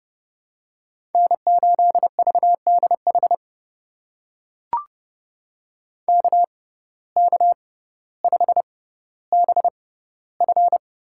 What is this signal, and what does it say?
Morse code, heard as a keyed tone, about 30 wpm, spaced as for 15 wpm, 700 Hz. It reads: N8VDH KK5BF
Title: USA Call Signs - 6x Character Spacing - 30wpm